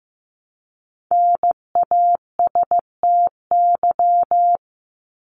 Morse code, heard as a keyed tone, about 15 wpm, 700 Hz. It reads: NASTY
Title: Top 1751-2000 Words 15wpm